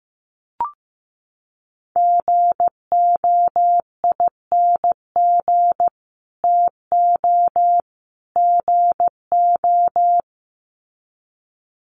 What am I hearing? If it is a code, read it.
Morse code, heard as a keyed tone, about 15 wpm, 700 Hz. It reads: GOING TO GO
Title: Sets of 3 Words 15wpm